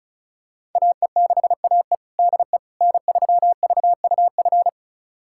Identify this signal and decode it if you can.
Morse code, heard as a keyed tone, about 34 wpm, 690 Hz. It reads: AE6AE DE N3VUF
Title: USA Callsign DE Callsign - 5 Characters - 35wpm